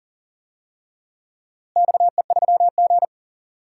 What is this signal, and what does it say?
Morse code, heard as a keyed tone, about 40 wpm, 700 Hz. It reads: XE3G
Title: DXCC-100 Call Signs - 4 Characters - Repeated 2x - 40wpm